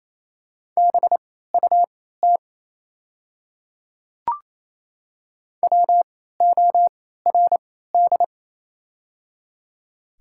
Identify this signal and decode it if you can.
Morse code, heard as a keyed tone, about 28 wpm, spaced as for 16 wpm, 700 Hz. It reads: BUT WORD
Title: 3x Character Spacing - Top 100 Words - 28wpm